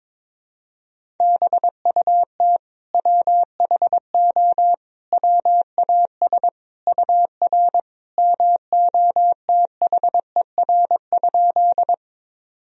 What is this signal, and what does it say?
Morse code, heard as a keyed tone, about 22 wpm, 700 Hz. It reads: BUT WHO WAS UR MOTHER?
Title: Sentences from Top 200 Words and Abbreviations 22wpm